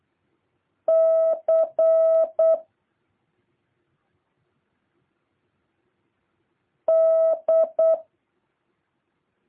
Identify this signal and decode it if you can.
Morse code, heard as a keyed tone, about 8 wpm, 650 Hz. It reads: C D
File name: jh8cbh / 2023cw